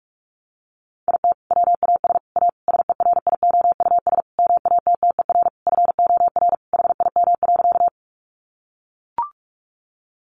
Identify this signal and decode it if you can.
Morse code, heard as a keyed tone, about 44 wpm, 700 Hz. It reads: IT WAS A SERIOUS MATTER FOR HIM.
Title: Sentences from Top 1100 Words 45wpm